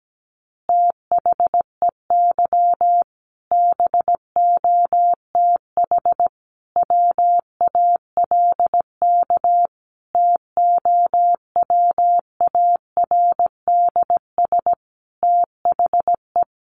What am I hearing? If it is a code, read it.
Morse code, heard as a keyed tone, about 17 wpm, 700 Hz. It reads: THEY BOTH WALK TOWARDS THE
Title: Additional Easy Phrases 17wpm